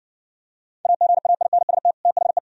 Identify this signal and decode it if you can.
Morse code, heard as a keyed tone, about 60 wpm, 690 Hz. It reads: AGAINST THE